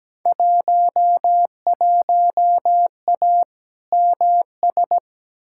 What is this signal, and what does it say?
Morse code, heard as a keyed tone, about 17 wpm, 700 Hz. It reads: 11A MS
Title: Field Day Exchanges Encore - 17wpm